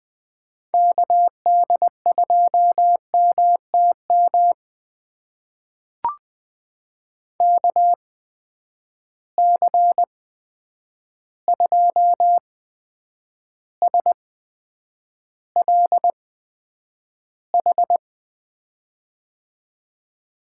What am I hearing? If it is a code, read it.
Morse code, heard as a keyed tone, about 20 wpm, spaced as for 9 wpm, 700 Hz. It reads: KD2MTM KC2SLH
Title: USA Call Signs - 6 Characters - 8x Character Spacing - 20wpm